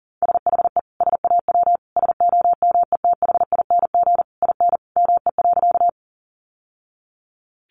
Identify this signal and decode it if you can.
Morse code, heard as a keyed tone, about 40 wpm, 700 Hz. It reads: SHE SAW SOMETHING IN ME.
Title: Sentences from Top 600 Words 40wpm